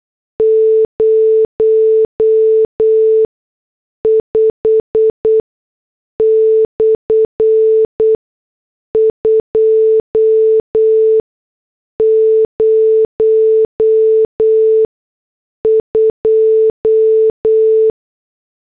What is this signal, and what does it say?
Morse code, heard as a keyed tone, about 8 wpm, spaced as for 6 wpm, 440 Hz. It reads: 05/202